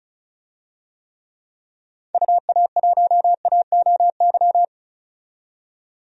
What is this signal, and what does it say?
Morse code, heard as a keyed tone, about 34 wpm, 700 Hz. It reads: UA1AOY
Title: DXCC-100 Call Signs - 6 Characters - 35wpm